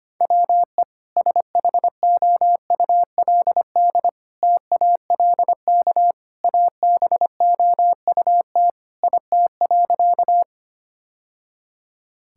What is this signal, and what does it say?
Morse code, heard as a keyed tone, about 25 wpm, 700 Hz. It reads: WE SHOULD TALK ABOUT IT.
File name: practice / 25wpm / Sentences with 2-Word Phrases - 25wpm